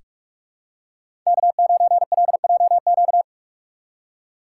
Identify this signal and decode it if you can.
Morse code, heard as a keyed tone, about 43 wpm, 700 Hz. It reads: K9LJQ
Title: USA Call Signs - 6x Character Spacing - 45wpm